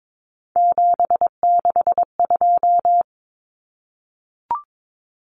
Morse code, heard as a keyed tone, about 22 wpm, 700 Hz. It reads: 762
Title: Common Cut Numbers - 3-Digit 22wpm